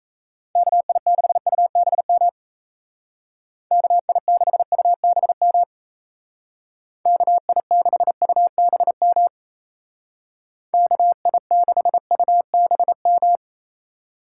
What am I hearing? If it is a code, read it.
Morse code, heard as a keyed tone, about 33 wpm, 700 Hz. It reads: KI6UBM KI6UBM KI6UBM KI6UBM
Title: Speed Racing - USA 2x3 Call Signs 28wpm